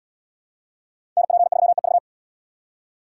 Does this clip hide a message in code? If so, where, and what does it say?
Yes: Morse code from 1.2 to 2.0 s, heard as a keyed tone, about 70 wpm, 700 Hz. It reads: NP3EV